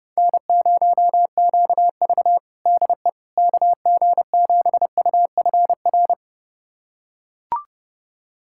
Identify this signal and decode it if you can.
Morse code, heard as a keyed tone, about 32 wpm, 700 Hz. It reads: N0QV DE KG7UFR
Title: USA Callsign DE Callsign 30wpm